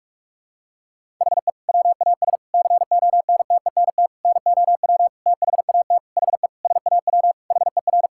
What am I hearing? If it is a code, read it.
Morse code, heard as a keyed tone, about 41 wpm, 700 Hz. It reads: HE WAS CONTENT NOW THAT HE SAW HER
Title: Sentences from Top 1000 Words - Review Part 6 - 45wpm